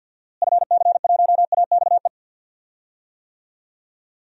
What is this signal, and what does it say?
Morse code, heard as a keyed tone, about 47 wpm, 700 Hz. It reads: FC1AXE
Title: DXCC-100 Call Signs - 6 Characters - Repeated 3x - 50wpm